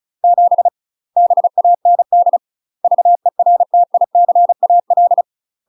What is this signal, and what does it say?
Morse code, heard as a keyed tone, about 36 wpm, 700 Hz. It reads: 7 BAND VERTICAL
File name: UK MASTER QSOs TRAINING / 35WPM / UK MASTER QSOs 0041